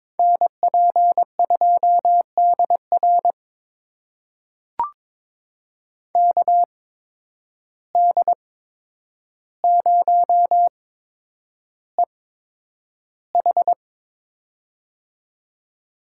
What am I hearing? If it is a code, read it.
Morse code, heard as a keyed tone, about 22 wpm, spaced as for 9 wpm, 700 Hz. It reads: NP2DR KD0EH
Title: USA Call Signs - 5 Characters - 8x Character Spacing - 22wpm